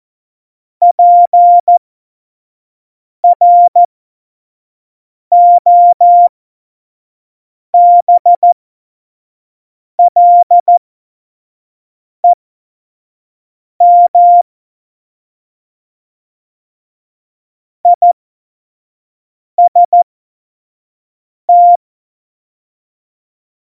Morse code, heard as a keyed tone, about 14 wpm, spaced as for 5 wpm, 700 Hz. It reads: PROBLEM IST
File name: 25 F70